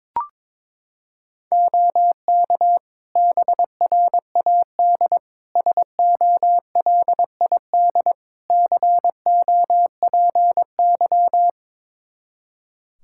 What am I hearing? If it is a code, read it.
Morse code, heard as a keyed tone, about 22 wpm, 700 Hz. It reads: OK BRAD SOLID COPY
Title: K1USN SST QSO Practice 22wpm